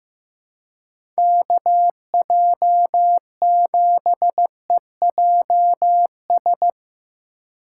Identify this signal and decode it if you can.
Morse code, heard as a keyed tone, about 15 wpm, 700 Hz. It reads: KJ7EJS
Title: USA Call Signs - 6 Characters - 8x Character Spacing - 15wpm